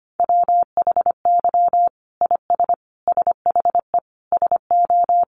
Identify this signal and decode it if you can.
Morse code, heard as a keyed tone, about 25 wpm, 700 Hz. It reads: WHY IS SHE SO
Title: Sentences from Top 800 Words 25wpm